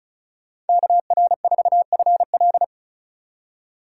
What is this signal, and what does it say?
Morse code, heard as a keyed tone, about 42 wpm, 700 Hz. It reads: KR4FL
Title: Speed Racing - USA 2x2 Call Signs 35wpm